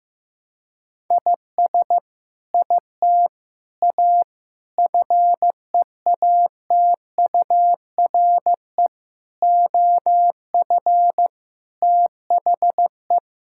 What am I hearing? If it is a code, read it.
Morse code, heard as a keyed tone, about 15 wpm, 700 Hz. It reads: IS IT A FEATURE OF THE
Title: Sentences from Top 1000 Words - Review Part 5 - 15wpm